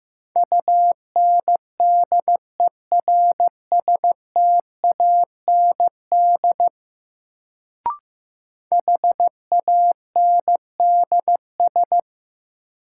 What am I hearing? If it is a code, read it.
Morse code, heard as a keyed tone, about 15 wpm, 700 Hz. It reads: UNDERSTAND HANDS
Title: Sets of 4 Words - New Word Drill - 15wpm